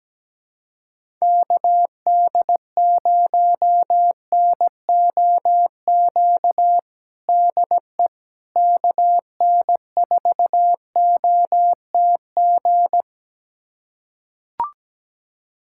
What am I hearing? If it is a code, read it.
Morse code, heard as a keyed tone, about 17 wpm, 700 Hz. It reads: KD0NOQ DE KN4OTG
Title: USA Callsign DE Callsign - 6 Characters - 17wpm